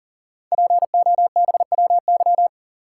Morse code, heard as a keyed tone, about 42 wpm, 690 Hz. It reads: POBWY
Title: Code Groups - 5 Letters - 40wpm